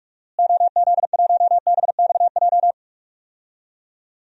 Morse code, heard as a keyed tone, about 46 wpm, 700 Hz. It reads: OZ1BXJ
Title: DXCC-100 Call Signs - 6 Characters - Repeated 3x - 45wpm